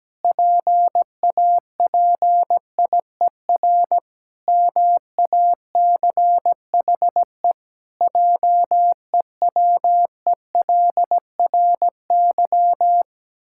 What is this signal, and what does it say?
Morse code, heard as a keyed tone, about 17 wpm, 700 Hz. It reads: PAPIER MACHE JEWELRY
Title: Pangram Challenge v2 - 17wpm